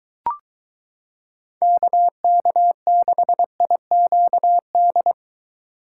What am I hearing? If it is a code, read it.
Morse code, heard as a keyed tone, about 23 wpm, 700 Hz. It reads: KK6IQD